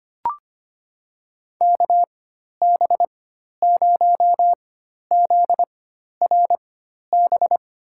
Morse code, heard as a keyed tone, about 25 wpm, spaced as for 12 wpm, 700 Hz. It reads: KB0ZRB
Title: USA Call Signs - 6 Characters - 4x Character Spacing - 25wpm